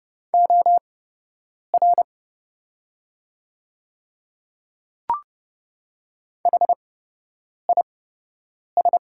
Morse code, heard as a keyed tone, about 30 wpm, spaced as for 8 wpm, 700 Hz. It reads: OR HIS
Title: 8x Character Spacing - Top 100 Words - 30wpm